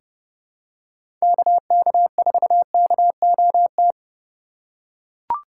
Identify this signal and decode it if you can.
Morse code, heard as a keyed tone, about 31 wpm, 700 Hz. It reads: KK4KOT
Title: USA Call Signs - 6 Characters - 4x Character Spacing - 30wpm